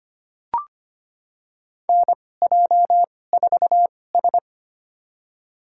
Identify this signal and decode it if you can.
Morse code, heard as a keyed tone, about 25 wpm, spaced as for 18 wpm, 700 Hz. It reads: NJ4S
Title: USA Call Signs - 4 Characters - 2x Character Spacing - 25wpm